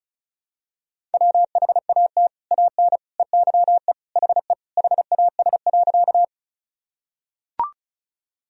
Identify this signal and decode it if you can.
Morse code, heard as a keyed tone, about 36 wpm, 700 Hz. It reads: WHAT AN EYE HE HAS.
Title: Sentences from Top 1000 Words - Review Part 1 - 35wpm